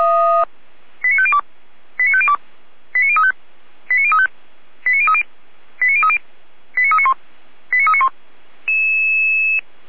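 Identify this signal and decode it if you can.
Five-tone selective call (ZVEI), heard as quick runs of five tones, tones of about 70 ms. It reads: 89502 89502 89035 89035 89030 89030 89391 89391
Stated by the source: ZVEI five-tone selective call received by SDR, Feuerwehr Orlamünde.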